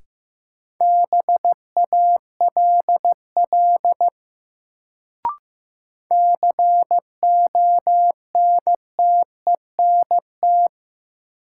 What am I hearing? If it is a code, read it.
Morse code, heard as a keyed tone, about 15 wpm, 700 Hz. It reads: BALL CONTENT
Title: Sentences from Top 900 Words - New Word Drill - 15wpm